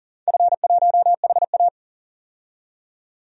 Morse code, heard as a keyed tone, about 40 wpm, 700 Hz. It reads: F1HA
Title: DXCC-100 Call Signs - 4 Characters - 40wpm